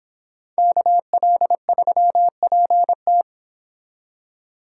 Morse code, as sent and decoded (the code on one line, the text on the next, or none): -.- .-.. ...-- .--. -
KL3PT